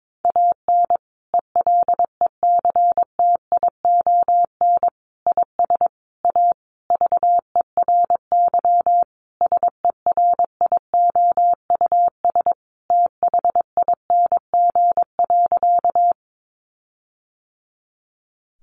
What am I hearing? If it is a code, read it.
Morse code, heard as a keyed tone, about 22 wpm, 700 Hz. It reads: AN ELECTION IS A VERY SERIOUS THING.
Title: Sentences from Top 1500 Words 22wpm